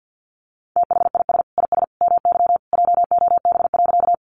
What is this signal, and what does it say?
Morse code, heard as a keyed tone, about 48 wpm, 700 Hz. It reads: THIS IS MY JOB.
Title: Sentences from Top 600 Words 50wpm